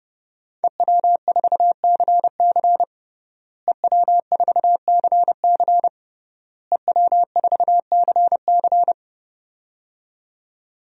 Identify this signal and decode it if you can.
Morse code, heard as a keyed tone, about 30 wpm, 700 Hz. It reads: EW4CC EW4CC EW4CC
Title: DXCC-100 Call Signs - 5 Characters - Repeated 3x - 30wpm